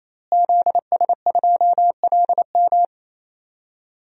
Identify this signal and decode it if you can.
Morse code, heard as a keyed tone, about 27 wpm, 700 Hz. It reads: ZS2LM
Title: DXCC-100 Call Signs - 5 Characters - Repeated 3x - 28wpm